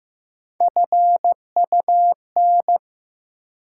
Morse code, heard as a keyed tone, about 15 wpm, 700 Hz. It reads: FUN
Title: Sentences from Top 700 Words - New Word Drill - 15wpm